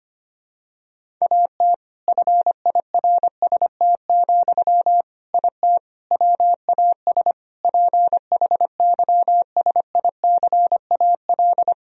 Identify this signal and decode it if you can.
Morse code, heard as a keyed tone, about 25 wpm, 700 Hz. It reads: AT FIRST, IT WAS PHYSICAL
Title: Sentences from Top 1000 Words - Review Part 4 - 25wpm